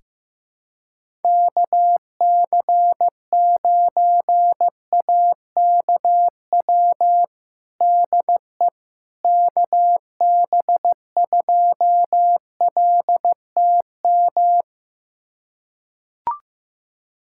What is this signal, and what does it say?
Morse code, heard as a keyed tone, about 15 wpm, 700 Hz. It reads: KC9AKW DE KB2LTM
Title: USA Callsign DE Callsign 15wpm